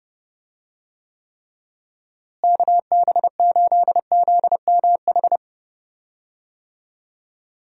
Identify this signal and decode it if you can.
Morse code, heard as a keyed tone, about 30 wpm, 700 Hz. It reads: KB8ZMH
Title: USA Call Signs - 6 Characters - Repeated 3x - 30wpm